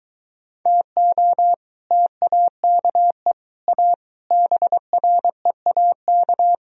Morse code, heard as a keyed tone, about 23 wpm, 700 Hz. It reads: TO TAKE A BREAK